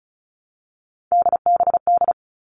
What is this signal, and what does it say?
Morse code, heard as a keyed tone, about 34 wpm, 700 Hz. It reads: DBD
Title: All Cut Numbers - 3-Digit 35wpm